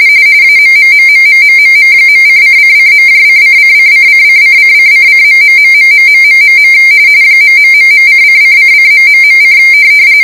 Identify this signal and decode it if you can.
RTTY, about 45.45 baud. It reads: AMMMMMMMM.::::.::::::::::::......... ....:::::...: